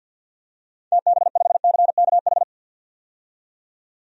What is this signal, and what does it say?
Morse code, heard as a keyed tone, about 45 wpm, 700 Hz. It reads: TB5CKH
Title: DXCC-100 Call Signs - 6 Characters - 50wpm